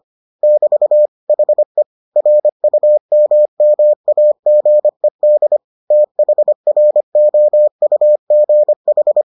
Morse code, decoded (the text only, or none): <BT> HE RUMMAGED THROUGH